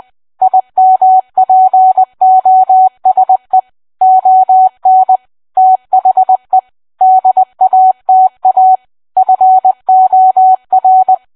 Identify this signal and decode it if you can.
Morse code, heard as a keyed tone, about 20 wpm, 750 Hz. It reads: IMPOSE ON THE DATA FOR